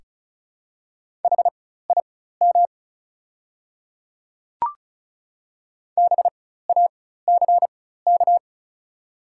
Morse code, heard as a keyed tone, about 36 wpm, spaced as for 16 wpm, 700 Hz. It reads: HIM BACK
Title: 4x Character Spacing - Top 300 Words - 35wpm